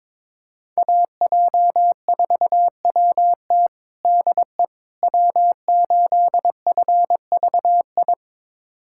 Morse code, heard as a keyed tone, about 21 wpm, 700 Hz. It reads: AJ4WT DE W8FVI
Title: USA Callsign DE Callsign - 5 Characters - 22wpm